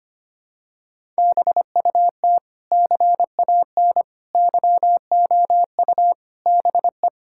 Morse code, heard as a keyed tone, about 25 wpm, 700 Hz. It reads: BUT CAN YOU BE
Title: Sentences from Top 1000 Words - Review Part 6 - 25wpm